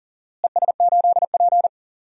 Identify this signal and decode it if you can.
Morse code, heard as a keyed tone, about 40 wpm, 700 Hz. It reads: ES8P
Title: DXCC-100 Call Signs - 4 Characters - Repeated 2x - 40wpm